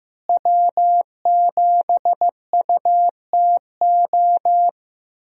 Morse code, heard as a keyed tone, about 15 wpm, 700 Hz. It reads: W7UTO
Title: USA Call Signs - 5 Characters - Repeated 3x - 15wpm